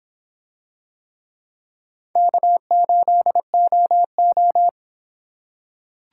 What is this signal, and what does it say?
Morse code, heard as a keyed tone, about 26 wpm, 700 Hz. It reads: K8OO